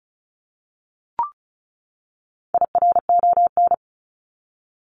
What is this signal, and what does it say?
Morse code, heard as a keyed tone, about 34 wpm, 700 Hz. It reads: IRON
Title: Top 1751-2000 Words 35wpm